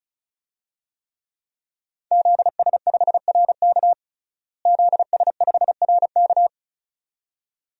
Morse code, heard as a keyed tone, about 36 wpm, 700 Hz. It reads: ZS5RK ZS5RK
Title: DXCC-100 Call Signs - 5 Characters - Repeated 2x - 35wpm